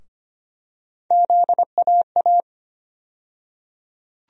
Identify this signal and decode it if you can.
Morse code, heard as a keyed tone, about 25 wpm, 700 Hz. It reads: ZAA
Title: USA Call Sign Postfixes - 3 Letters 25wpm